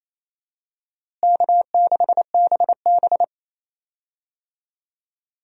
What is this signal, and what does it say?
Morse code, heard as a keyed tone, about 28 wpm, 700 Hz. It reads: K6BB
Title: USA Call Signs - 4 Characters - Repeated 3x - 28wpm